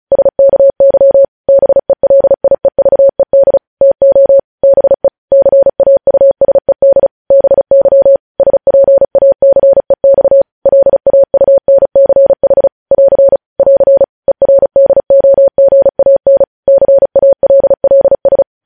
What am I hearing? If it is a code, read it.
Morse code, heard as a keyed tone, about 36 wpm, 550 Hz. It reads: SKY BELIEVED TO BE CAUSED BY SPACEX LAUNCH <AR> <AR> ERDOGAN CALLS
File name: podcast / News headlines-1x-35x35